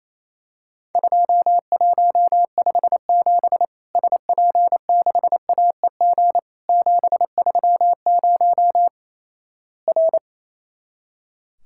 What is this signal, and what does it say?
Morse code, heard as a keyed tone, about 28 wpm, 700 Hz. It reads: 2157 SP6AEG 730 R
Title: Worked All Europe DX Contest - QTC Exchange - 28wpm